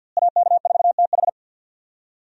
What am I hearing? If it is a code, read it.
Morse code, heard as a keyed tone, about 46 wpm, 700 Hz. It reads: AK4TH
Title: USA Call Signs - 5 Characters - Repeated 3x - 50wpm